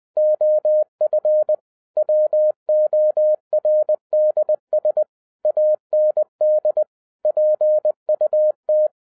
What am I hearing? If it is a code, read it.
Morse code, heard as a keyed tone, about 20 wpm, 610 Hz. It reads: OF WORDS AND PUT
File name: 2011-10-mp3 / n13-TIPS FROM MY 34 YEARS OF EXPERIENCE 20wpm